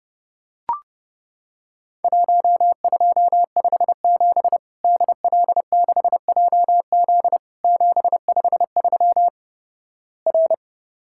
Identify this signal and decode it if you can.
Morse code, heard as a keyed tone, about 28 wpm, 700 Hz. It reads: 1257 DL6JZ 753 R
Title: Worked All Europe DX Contest - QTC Exchange - 30wpm